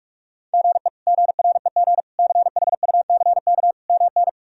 Morse code, heard as a keyed tone, about 46 wpm, 700 Hz. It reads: GE GREG CHUCK MN
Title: K1USN SST QSO Practice - 45wpm